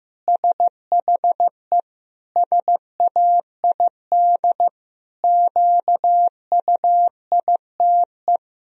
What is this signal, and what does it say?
Morse code, heard as a keyed tone, about 15 wpm, 700 Hz. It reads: SHE SAID QUITE
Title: Sentences from Top 1000 Words - Review Part 4 - 15wpm